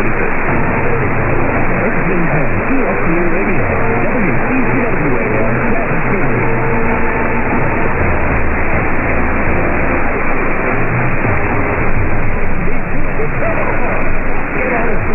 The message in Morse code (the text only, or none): none